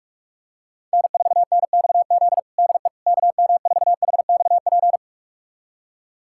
Morse code, heard as a keyed tone, about 46 wpm, 700 Hz. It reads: N4NXZ DE KM4HXP
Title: USA Callsign DE Callsign 45wpm